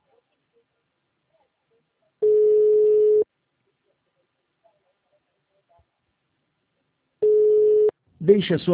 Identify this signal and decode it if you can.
Ringback tone (Europe, 425 Hz), with speech.